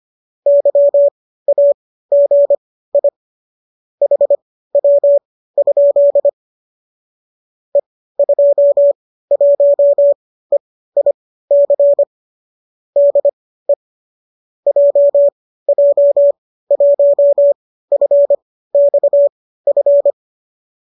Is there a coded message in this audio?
Yes: Morse code from 0.5 to 20.1 s, heard as a keyed tone, about 25 wpm, spaced as for 15 wpm, 570 Hz. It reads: YAGI HW? E21EIC DE JJ1FXF